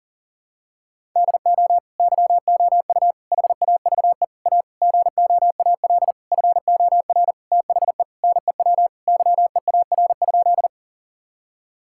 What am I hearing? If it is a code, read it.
Morse code, heard as a keyed tone, about 40 wpm, 700 Hz. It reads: DO YOU HAVE A GOAL FOR THE NEW YEAR?
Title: Sentences from Top 1000 Words - Review Part 5 - 40wpm